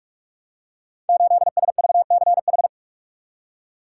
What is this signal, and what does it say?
Morse code, heard as a keyed tone, about 44 wpm, 700 Hz. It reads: 8SVCH
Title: Code Groups - 5 Letter:Numbers - 45wpm